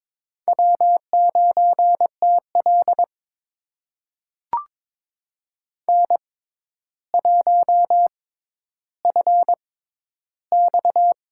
Morse code, heard as a keyed tone, about 23 wpm, spaced as for 11 wpm, 700 Hz. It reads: W9TL N1FX
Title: USA Call Signs - 4 Characters - 6x Character Spacing - 22wpm